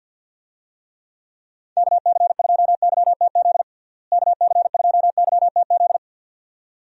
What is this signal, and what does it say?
Morse code, heard as a keyed tone, about 51 wpm, 700 Hz. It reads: KC2YTZ KC2YTZ
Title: USA Call Signs - 6 Characters - Repeated 2x - 50wpm